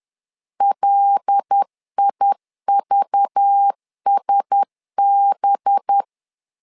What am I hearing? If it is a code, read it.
Morse code, heard as a keyed tone, about 10 wpm, 800 Hz. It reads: LIVSB
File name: LEZIONI / 5 LBVC / 5 LBV 10wpm 2248